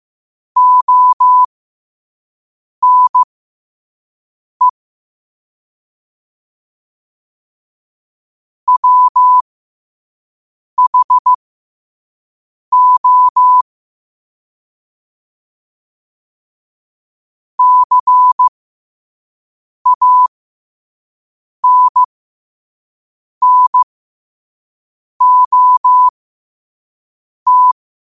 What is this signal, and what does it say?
Morse code, heard as a keyed tone, about 15 wpm, spaced as for 5 wpm, 1000 Hz. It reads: ONE WHO CANNOT
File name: qotdfiles / 5wpm